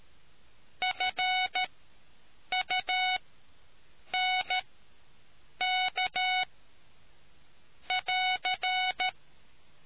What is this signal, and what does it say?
Morse code, heard as a keyed tone, about 13 wpm, spaced as for 7 wpm, 740 Hz. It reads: FUNK <AR>